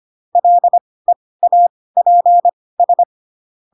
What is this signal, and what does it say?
Morse code, heard as a keyed tone, about 25 wpm, spaced as for 18 wpm, 700 Hz. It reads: LEAPS
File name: Pinocchio / Pinocchio ch6 12 25 ik1hge